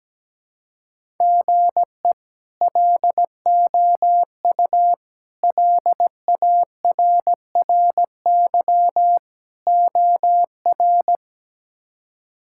Morse code, heard as a keyed tone, about 17 wpm, 700 Hz. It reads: GE LOU LARRY OR
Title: K1USN SST QSO Practice - 17wpm